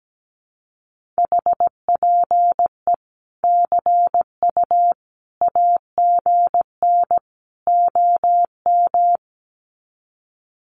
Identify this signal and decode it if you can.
Morse code, heard as a keyed tone, about 17 wpm, 700 Hz. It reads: HPE CU AGN OM
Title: Common QSO Phrases 17wpm